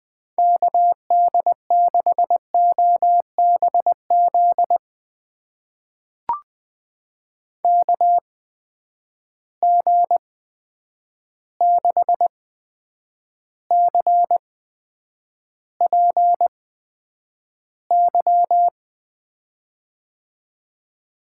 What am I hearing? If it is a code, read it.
Morse code, heard as a keyed tone, about 20 wpm, spaced as for 9 wpm, 700 Hz. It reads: KD6OBZ KG6CPY
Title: USA Call Signs - 6 Characters - 8x Character Spacing - 20wpm